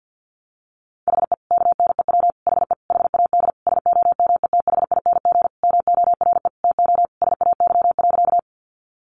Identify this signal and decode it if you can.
Morse code, heard as a keyed tone, about 52 wpm, 700 Hz. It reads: HE KNEW HE HAD SOMETHING MORE TO SAY.
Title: Sentences from Top 600 Words 50wpm